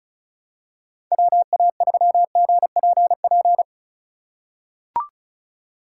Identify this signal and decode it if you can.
Morse code, heard as a keyed tone, about 34 wpm, 700 Hz. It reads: WA3GPP
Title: K1USN SST QSO Practice - 35wpm